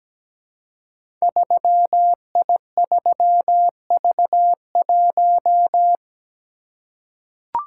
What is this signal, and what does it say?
Morse code, heard as a keyed tone, about 17 wpm, 700 Hz. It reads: 3I3V1
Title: Code Groups - 5 Letter:Numbers - 17wpm